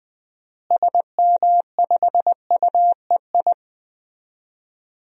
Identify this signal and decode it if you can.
Morse code, heard as a keyed tone, about 20 wpm, 700 Hz. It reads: SM5UEI
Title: DXCC-100 Call Signs - 6 Characters - Repeated 2x - 20wpm